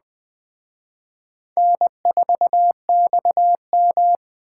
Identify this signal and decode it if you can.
Morse code, heard as a keyed tone, about 20 wpm, 700 Hz. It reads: N4XM
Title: Speed Racing - USA 1x2 Call Signs 17wpm